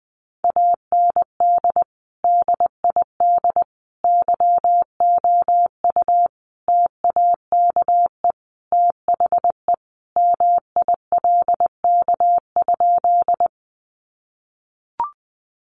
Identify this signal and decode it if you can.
Morse code, heard as a keyed tone, about 20 wpm, 700 Hz. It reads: AND DID YOU TAKE THE MILK?
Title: Sentences from Top 1500 Words 20wpm